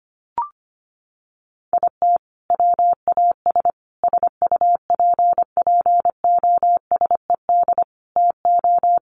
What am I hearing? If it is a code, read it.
Morse code, heard as a keyed tone, about 25 wpm, 700 Hz. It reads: IT WAS SUPPOSED TO
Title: Sets of 5 Words Encore 25wpm